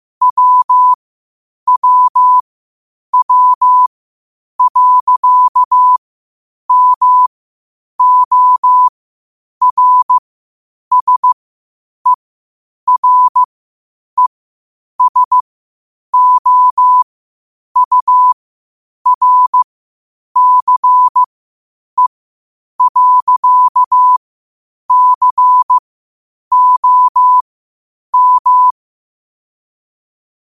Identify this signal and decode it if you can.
Morse code, heard as a keyed tone, about 15 wpm, spaced as for 8 wpm, 1000 Hz. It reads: WWW.MORSERESOURCE.COM
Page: New quotes every day in morse code at 8 Words per minute.